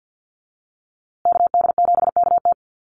Morse code, heard as a keyed tone, about 50 wpm, 700 Hz. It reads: KD7KT
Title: USA Call Signs Repeated 2x 50wpm